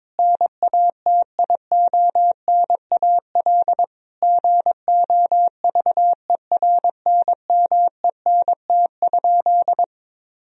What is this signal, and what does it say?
Morse code, heard as a keyed tone, about 22 wpm, 700 Hz. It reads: NATIONAL GOVERNMENT?
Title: Sentences from Top 1000 Words - Review Part 4 - 22wpm